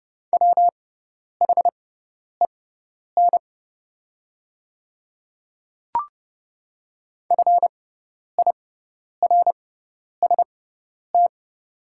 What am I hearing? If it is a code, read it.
Morse code, heard as a keyed tone, about 31 wpm, spaced as for 10 wpm, 700 Hz. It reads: WHEN FIRST
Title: 6x Character Spacing - Top 300 Words - 30wpm